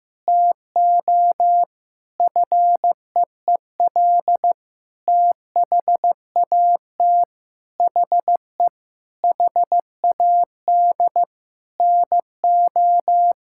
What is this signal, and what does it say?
Morse code, heard as a keyed tone, about 15 wpm, 700 Hz. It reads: TO FEEL THAT HE HAD NO
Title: Sentences from Top 1000 Words - Review Part 5 - 15wpm